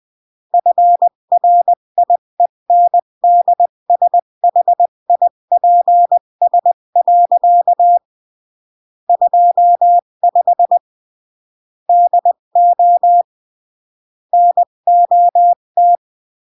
Morse code, heard as a keyed tone, about 20 wpm, 700 Hz. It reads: FRIENDSHIPS. 25 DO NOT
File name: CW / jodorowsky / 82 commandments 20-15